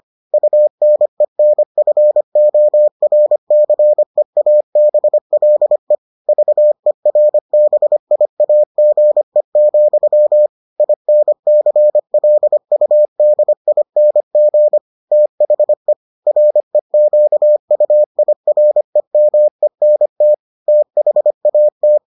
Morse code, heard as a keyed tone, about 25 wpm, 600 Hz. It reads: UNENFORCEABLE VERBIAGE, INCLUDING THE REQUIREMENT THAT